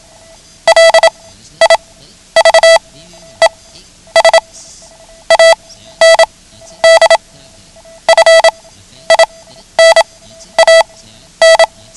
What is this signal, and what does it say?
Morse code, heard as a keyed tone, about 27 wpm, spaced as for 12 wpm, 660 Hz. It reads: LIVES AND FINAN
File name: ham radio / CodeQuick / Code Quick 09